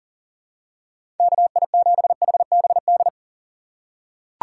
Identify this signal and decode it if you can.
Morse code, heard as a keyed tone, about 39 wpm, 700 Hz. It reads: KI7HBD
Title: USA Call Signs - 6 Characters - 4x Character Spacing - 40wpm